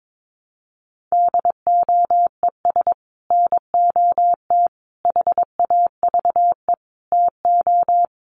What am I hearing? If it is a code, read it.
Morse code, heard as a keyed tone, about 22 wpm, 700 Hz. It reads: DOES NOT HAVE TO
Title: Sets of 5 Words 22wpm